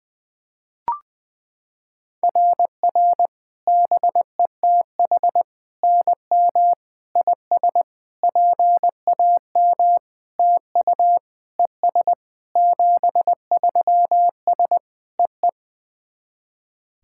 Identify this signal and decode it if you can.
Morse code, heard as a keyed tone, about 21 wpm, 700 Hz. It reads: RR BETH NM IS PAM TU ES 73S EE
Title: SOTA QSO Practice 20wpm